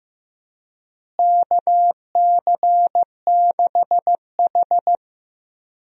Morse code, heard as a keyed tone, about 15 wpm, 700 Hz. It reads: KC6H
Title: USA Call Signs - 4 Characters - Repeated 2x - 15wpm